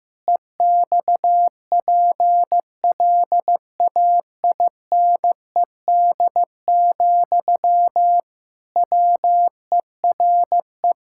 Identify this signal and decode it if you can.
Morse code, heard as a keyed tone, about 16 wpm, 700 Hz. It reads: EXPLAINED, WERE
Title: Pangram Challenge v2 - 15wpm